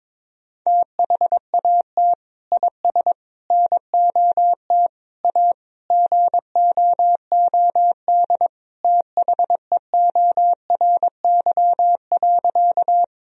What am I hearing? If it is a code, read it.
Morse code, heard as a keyed tone, about 22 wpm, 700 Hz. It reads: THAT IS NOT A GOOD THEORY.
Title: Sentences from Top 1000 Words - Review Part 3 - 22wpm